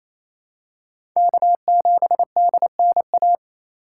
Morse code, heard as a keyed tone, about 28 wpm, 700 Hz. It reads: K7DNA
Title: USA Call Signs - 5 Characters - 4x Character Spacing - 28wpm